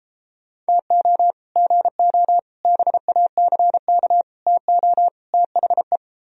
Morse code, heard as a keyed tone, about 29 wpm, 700 Hz. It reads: TO GO BACK TO THE